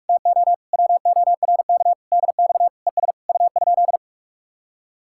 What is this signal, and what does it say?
Morse code, heard as a keyed tone, about 55 wpm, 700 Hz. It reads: TO WORK DX ES U?
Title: Speed-Racing Common QSO Phrases - 45wpm